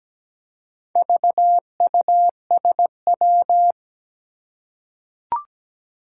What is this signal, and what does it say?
Morse code, heard as a keyed tone, about 17 wpm, 700 Hz. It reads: VUSW
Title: Four Challenging Letter-Number Combinations 17wpm